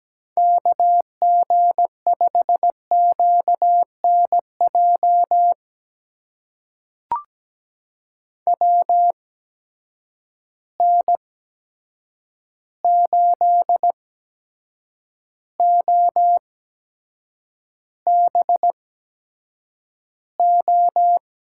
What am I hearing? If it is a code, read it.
Morse code, heard as a keyed tone, about 17 wpm, spaced as for 7 wpm, 700 Hz. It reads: KG5QNJ WN8OBO
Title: USA Call Signs - 8x Character Spacing - 17wpm